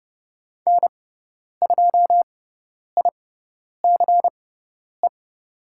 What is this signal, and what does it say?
Morse code, heard as a keyed tone, about 30 wpm, spaced as for 10 wpm, 700 Hz. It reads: N2ICE
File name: Lesson 16g - Letter c - Call Signs 30wpm